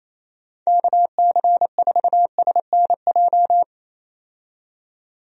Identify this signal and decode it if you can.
Morse code, heard as a keyed tone, about 28 wpm, 700 Hz. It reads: KC4SNJ
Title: USA Call Signs - 6 Characters - 28wpm